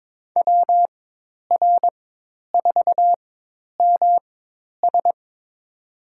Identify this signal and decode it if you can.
Morse code, heard as a keyed tone, about 22 wpm, spaced as for 10 wpm, 700 Hz. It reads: WR4MS
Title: USA Call Signs - 5 Characters - 4x Character Spacing - 22wpm